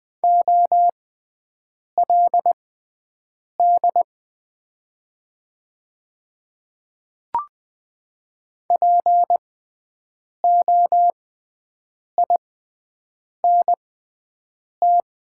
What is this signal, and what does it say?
Morse code, heard as a keyed tone, about 20 wpm, spaced as for 7 wpm, 700 Hz. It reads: OLD POINT
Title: 6x Character Spacing - Top 300 Words - 20wpm